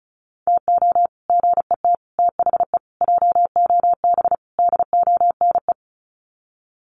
Morse code, heard as a keyed tone, about 34 wpm, 700 Hz. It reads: TO GET THE JOB DONE
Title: Sets of 5 Words Encore2 35wpm